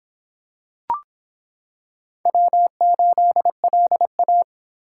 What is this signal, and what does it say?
Morse code, heard as a keyed tone, about 26 wpm, 700 Hz. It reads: W8LA